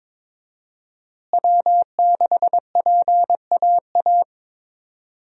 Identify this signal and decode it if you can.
Morse code, heard as a keyed tone, about 22 wpm, 700 Hz. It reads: W6PAA
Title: USA Call Signs - 5 Characters - Repeated 2x - 22wpm